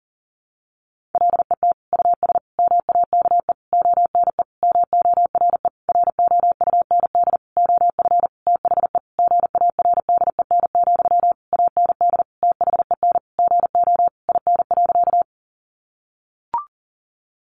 Morse code, heard as a keyed tone, about 40 wpm, 700 Hz. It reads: LET US MAKE ONE MORE ROUND OF THE GARDEN, AND THEN GO IN.
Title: Sentences from Top 700 Words 40wpm